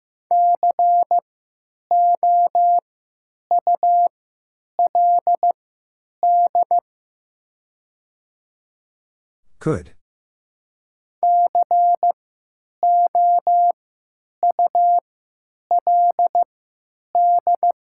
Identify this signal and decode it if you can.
Morse code, heard as a keyed tone, about 15 wpm, spaced as for 9 wpm, 700 Hz. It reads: COULD COULD
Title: 3x Character Spacing - Top 100 Words - 15wpm